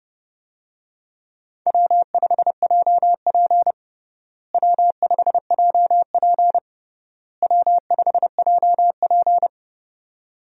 Morse code, heard as a keyed tone, about 30 wpm, 700 Hz. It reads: W5JP W5JP W5JP
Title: USA Call Signs - 4 Characters - Repeated 3x - 30wpm